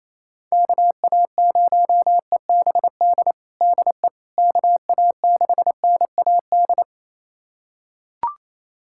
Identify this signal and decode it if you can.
Morse code, heard as a keyed tone, about 28 wpm, 700 Hz. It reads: KA0EBD DE KA6NAD
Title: USA Callsign DE Callsign - 6 Characters - 28wpm